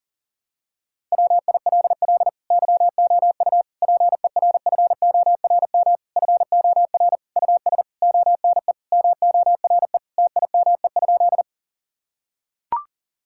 Morse code, heard as a keyed tone, about 40 wpm, 700 Hz. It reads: WILL YOU PERFORM FOR US ONE MORE TIME?
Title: Sentences from Top 1000 Words - Review Part 5 - 40wpm